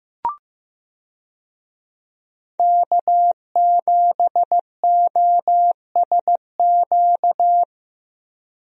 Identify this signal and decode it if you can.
Morse code, heard as a keyed tone, about 15 wpm, 700 Hz. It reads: K7OSQ
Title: USA Call Signs - 5 Characters - 15wpm